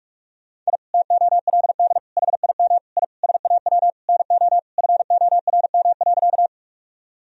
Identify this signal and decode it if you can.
Morse code, heard as a keyed tone, about 46 wpm, 700 Hz. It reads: I TOLD HIM I SAW NO FORM.
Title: Sentences from Top 1000 Words - Review Part 1 - 45wpm